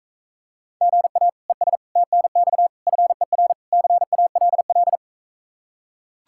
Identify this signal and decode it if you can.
Morse code, heard as a keyed tone, about 45 wpm, 700 Hz. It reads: GA ES TNX FER CALL